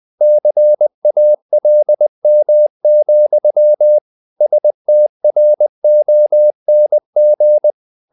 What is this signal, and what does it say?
Morse code, heard as a keyed tone, about 20 wpm, 600 Hz. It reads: CALM, STRONG
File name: the old man and the sea part 2 20wpm